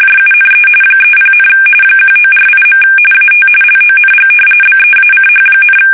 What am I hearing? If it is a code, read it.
RTTY, about 45.45 baud. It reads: JUNE 30, 1948. AT LONG BEACH,
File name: fsk850 / battleshipnj